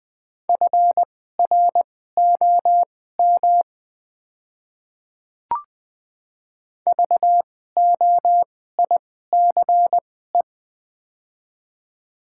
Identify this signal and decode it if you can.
Morse code, heard as a keyed tone, about 20 wpm, spaced as for 14 wpm, 700 Hz. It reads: FROM VOICE
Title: 2x Character Spacing - Top 500 Words - 20wpm